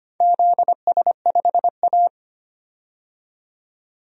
Morse code, heard as a keyed tone, about 25 wpm, 700 Hz. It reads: ZS5A
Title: DXCC-100 Call Signs - 4 Characters - 25wpm